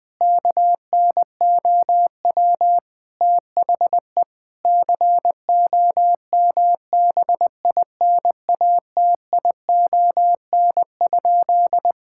Morse code, heard as a keyed tone, about 20 wpm, 700 Hz. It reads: KNOW THE COMBINATION?
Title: Sentences from Top 1000 Words - Review Part 6 - 20wpm